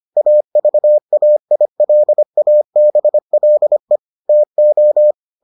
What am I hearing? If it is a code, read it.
Morse code, heard as a keyed tone, about 25 wpm, 600 Hz. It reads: AVAILABLE TO